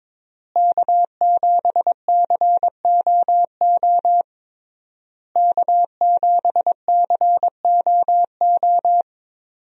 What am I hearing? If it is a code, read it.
Morse code, heard as a keyed tone, about 22 wpm, 700 Hz. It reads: K7COO K7COO
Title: USA Call Signs - 5 Characters - Repeated 3x - 22wpm